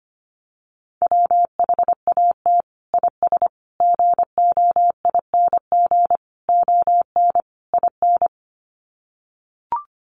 Morse code, heard as a keyed tone, about 25 wpm, 700 Hz. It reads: WHAT IS GOING ON IN
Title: Sets of 5 Words Encore2 25wpm